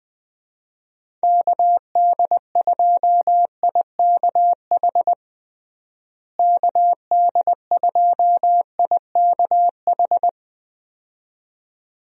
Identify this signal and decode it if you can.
Morse code, heard as a keyed tone, about 21 wpm, 700 Hz. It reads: KD2IKH KD2IKH
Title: USA Call Signs - 6 Characters - Repeated 2x - 20wpm